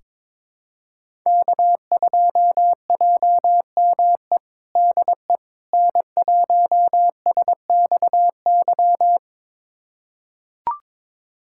Morse code, heard as a keyed tone, about 22 wpm, 700 Hz. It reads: K2JME DE N1SXY
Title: USA Callsign DE Callsign - 5 Characters - 22wpm